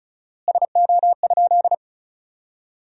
Morse code, heard as a keyed tone, about 36 wpm, 700 Hz. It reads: SO?
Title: 4x Word-Spacing - Sentences from Top 200 Words - 35wpm